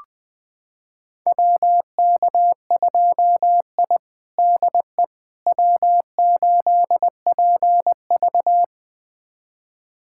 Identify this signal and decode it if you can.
Morse code, heard as a keyed tone, about 21 wpm, 700 Hz. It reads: WK2I DE W8PV
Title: USA Callsign DE Callsign - 4 Characters - 20wpm